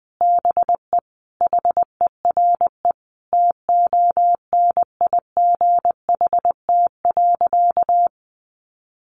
Morse code, heard as a keyed tone, about 20 wpm, 700 Hz. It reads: BE HERE TONIGHT.
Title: Sentences from Top 600 Words 20wpm